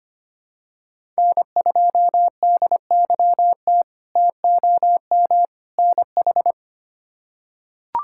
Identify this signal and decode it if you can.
Morse code, heard as a keyed tone, about 25 wpm, 700 Hz. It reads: N2DYT TOM NH
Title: K1USN SST QSO Practice - 25wpm